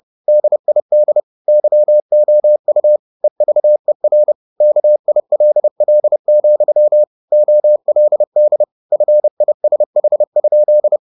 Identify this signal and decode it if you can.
Morse code, heard as a keyed tone, about 30 wpm, 600 Hz. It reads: DID YOU EVER KILL, OLD FISH?